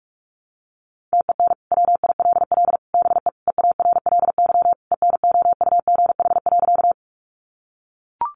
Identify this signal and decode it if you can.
Morse code, heard as a keyed tone, about 44 wpm, 700 Hz. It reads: TEN WILL BE EARLY ENOUGH.
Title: Sentences from Top 400 Words 45wpm